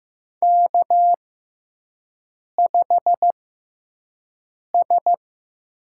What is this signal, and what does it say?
Morse code, heard as a keyed tone, about 15 wpm, spaced as for 5 wpm, 700 Hz. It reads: K5S
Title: USA Call Signs - 5 Characters - 6x Character Spacing - 15wpm